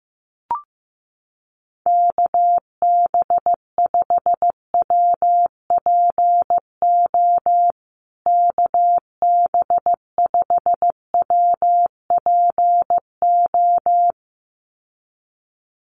Morse code, heard as a keyed tone, about 15 wpm, 700 Hz. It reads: KB5WPO KB5WPO
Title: USA Call Signs Repeated 2x 15wpm